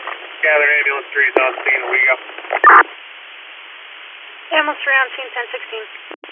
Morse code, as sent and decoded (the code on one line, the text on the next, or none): none